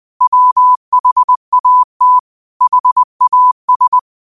Morse code, heard as a keyed tone, about 20 wpm, 1000 Hz. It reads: WHAT HAS